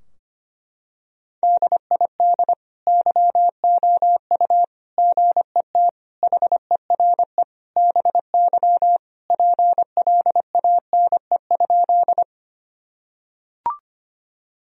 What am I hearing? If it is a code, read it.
Morse code, heard as a keyed tone, about 25 wpm, 700 Hz. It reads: DID YOU GET HERE BY PLANE?
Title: Sentences from Top 1000 Words - Review Part 3 - 25wpm